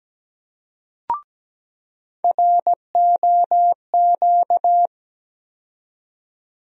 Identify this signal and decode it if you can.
Morse code, heard as a keyed tone, about 17 wpm, 700 Hz. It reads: ROQ
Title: USA Call Sign Postfixes - 3 Letters 17wpm